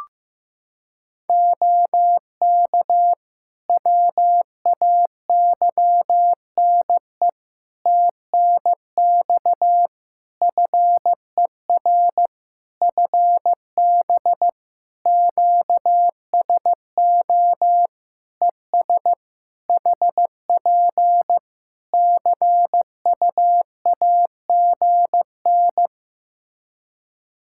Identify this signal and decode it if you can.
Morse code, heard as a keyed tone, about 14 wpm, 700 Hz. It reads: OK WAYNE TNX FER FB QSO ES HP CUAGN
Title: Practice QSO - Exchange 4 - Ending - 15wpm